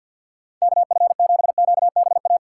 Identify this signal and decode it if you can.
Morse code, heard as a keyed tone, about 47 wpm, 700 Hz. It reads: KF7QBA
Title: USA Call Signs - 6x Character Spacing - 50wpm